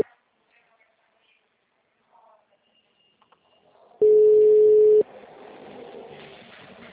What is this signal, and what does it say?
Ringback tone (Europe, 425 Hz).